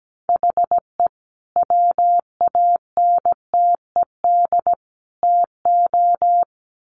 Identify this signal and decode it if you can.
Morse code, heard as a keyed tone, about 17 wpm, 700 Hz. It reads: HE WANTED TO
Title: Sets of 3 Words 17wpm